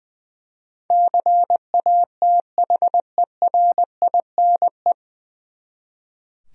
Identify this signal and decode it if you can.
Morse code, heard as a keyed tone, about 20 wpm, 700 Hz. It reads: CATHERINE
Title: USA Common Names - 20wpm